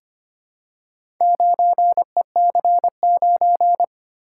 Morse code, heard as a keyed tone, about 25 wpm, 690 Hz. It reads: 9EC9
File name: practice / 25wpm / Four Letter-Number Combinations - 25wpm